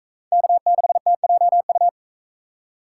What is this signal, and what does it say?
Morse code, heard as a keyed tone, about 39 wpm, 700 Hz. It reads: K6TJU